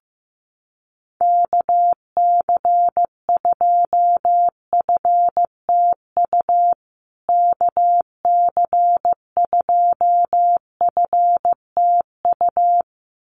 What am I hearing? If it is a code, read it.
Morse code, heard as a keyed tone, about 15 wpm, 700 Hz. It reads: KC2FTU KC2FTU
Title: USA Call Signs Repeated 2x 15wpm